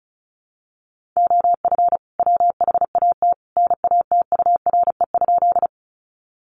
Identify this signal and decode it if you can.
Morse code, heard as a keyed tone, about 34 wpm, 700 Hz. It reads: OF WHAT NATURE?
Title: Sentences from Top 600 Words 35wpm